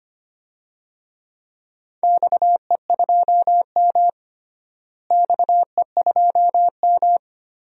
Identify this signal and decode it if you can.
Morse code, heard as a keyed tone, about 25 wpm, 700 Hz. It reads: XE2M XE2M
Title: DXCC-100 Call Signs - 4 Characters - Repeated 3x - 25wpm